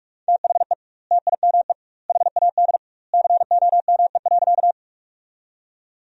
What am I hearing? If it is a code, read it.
Morse code, heard as a keyed tone, about 43 wpm, 700 Hz. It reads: THE TIME HAD COME.
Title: 2x Word-Spacing - Sentences from Top 100 Words - 45wpm